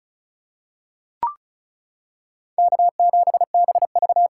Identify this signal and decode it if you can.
Morse code, heard as a keyed tone, about 36 wpm, 700 Hz. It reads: K7BV
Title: K1USN SST QSO Practice 35wpm